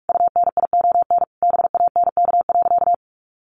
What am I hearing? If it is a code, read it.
Morse code, heard as a keyed tone, about 44 wpm, 700 Hz. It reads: UNION BANK.
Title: Sentences from Top 1400 Words 45wpm